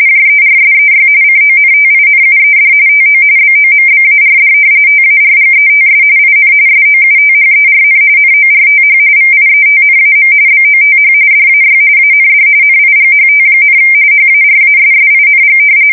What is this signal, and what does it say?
RTTY, about 45.45 baud. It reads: JERSEY CONTINUED TO BOMBARD TARGETS IN SOUTHERN NORTH VIETNAM BOTH DAY AND NIGHT, HARASSING
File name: fsk170 / battleshipnj